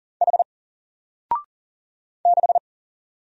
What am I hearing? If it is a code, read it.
Morse code, heard as a keyed tone, about 40 wpm, 700 Hz. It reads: H 6
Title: Single Letter-Number Alt 40wpm